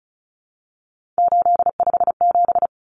Morse code, heard as a keyed tone, about 34 wpm, 700 Hz. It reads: 857
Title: Numbers - 3-Digit 35wpm